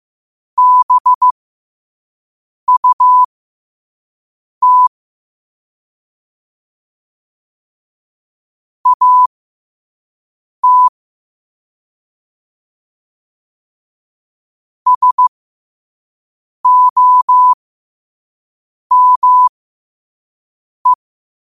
Morse code, heard as a keyed tone, about 15 wpm, spaced as for 5 wpm, 1000 Hz. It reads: BUT AT SOME
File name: qotdfiles / 5wpm